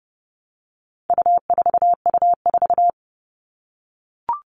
Morse code, heard as a keyed tone, about 30 wpm, 700 Hz. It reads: U4U4
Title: All Cut Numbers - 4-Digit 30wpm